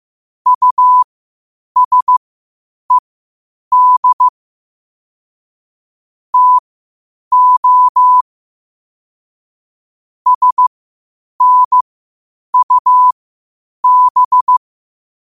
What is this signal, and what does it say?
Morse code, heard as a keyed tone, about 15 wpm, spaced as for 8 wpm, 1000 Hz. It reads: USED TO SNUB
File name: qotdfiles / 8wpm